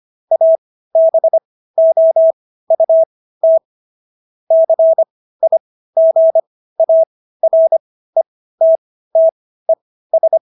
Morse code, heard as a keyed tone, about 25 wpm, spaced as for 15 wpm, 650 Hz. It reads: ABOUT CIGARETTES